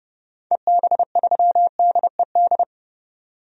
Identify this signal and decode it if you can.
Morse code, heard as a keyed tone, about 30 wpm, 700 Hz. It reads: EB3DED
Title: DXCC-100 Call Signs - 6 Characters - 30wpm